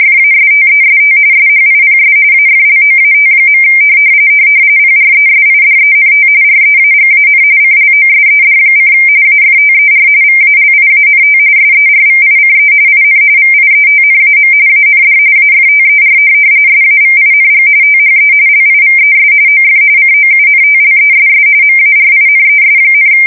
RTTY, about 45.45 baud. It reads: FASTEST SHIPS AFLOAT. PRESIDENT REAGAN CLOSED: AFTER VALIANT SERVICE IN VIETNAM AND AFTER SAVING THE LIVES OF COUNTLESS MARINES,
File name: fsk170 / USS-New-Jersey